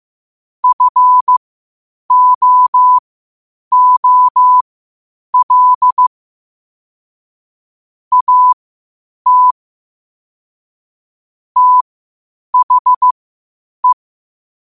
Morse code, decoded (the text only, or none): FOOL AT THE